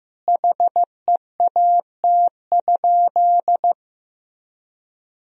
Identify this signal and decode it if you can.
Morse code, heard as a keyed tone, about 16 wpm, 700 Hz. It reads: HEAT?
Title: Sentences from Top 1000 Words - Review Part 3 - 15wpm